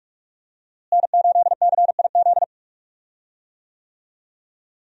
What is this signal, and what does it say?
Morse code, heard as a keyed tone, about 45 wpm, 700 Hz. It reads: N8CIZ
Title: USA Call Signs - 5 Characters - Repeated 2x - 45wpm